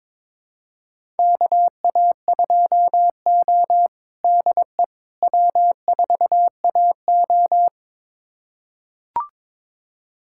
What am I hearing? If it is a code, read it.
Morse code, heard as a keyed tone, about 22 wpm, 700 Hz. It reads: KA2O DE W4AO
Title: USA Callsign DE Callsign - 4 Characters - 22wpm